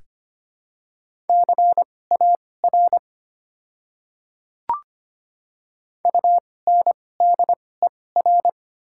Morse code, heard as a keyed tone, about 25 wpm, spaced as for 18 wpm, 700 Hz. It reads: CAR UNDER
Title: 2x Character Spacing - Top 300 Words - 25wpm